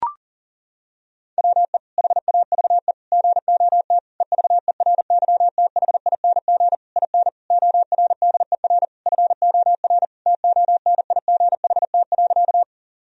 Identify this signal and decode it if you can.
Morse code, heard as a keyed tone, about 37 wpm, 700 Hz. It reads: WE HAVE GOT EVERYTHING IN ORDER FOR TONIGHT.
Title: Sentences from Top 1000 Words - Review Part 3 - 40wpm